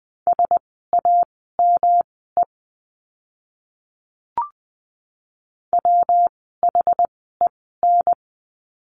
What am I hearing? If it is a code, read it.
Morse code, heard as a keyed tone, about 20 wpm, spaced as for 15 wpm, 700 Hz. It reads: SAME WHEN
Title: 115 Lesson 19f - Letter w - Words 20wpm